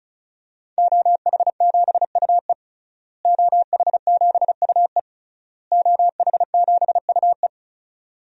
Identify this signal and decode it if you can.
Morse code, heard as a keyed tone, about 34 wpm, 700 Hz. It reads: OH7UE OH7UE OH7UE
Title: DXCC-100 Call Signs - 5 Characters - Repeated 3x - 35wpm